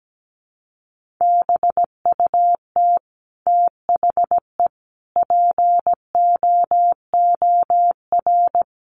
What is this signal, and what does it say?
Morse code, heard as a keyed tone, about 17 wpm, 700 Hz. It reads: BUT THE POOR
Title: Sentences from Top 1000 Words 17wpm